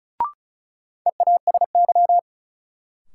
Morse code, heard as a keyed tone, about 36 wpm, 700 Hz. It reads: EASY
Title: Sentences from Top 700 Words - New Word Drill - 35wpm